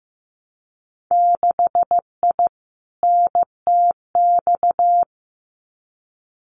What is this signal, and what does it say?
Morse code, heard as a keyed tone, about 15 wpm, 700 Hz. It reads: 6I NTX
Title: Winter Field Day Exchanges Encore 15wpm